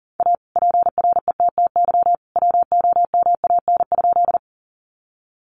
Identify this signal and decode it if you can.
Morse code, heard as a keyed tone, about 40 wpm, 700 Hz. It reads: A PRETTY WOMAN?
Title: Sentences from Top 1100 Words 40wpm